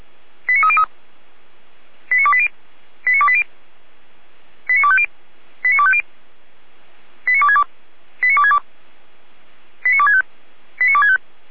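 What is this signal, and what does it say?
Five-tone selective call (ZVEI), heard as quick runs of five tones, tones of about 70 ms. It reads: 89303 89280 89280 89250 89250 89262 89262 89265 89265
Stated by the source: ZVEI five-tone selective call received on an SDR.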